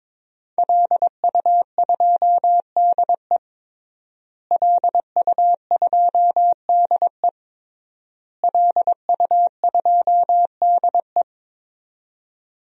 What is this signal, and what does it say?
Morse code, heard as a keyed tone, about 22 wpm, 700 Hz. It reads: LU2DE LU2DE LU2DE
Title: DXCC-100 Call Signs - 5 Characters - Repeated 3x - 22wpm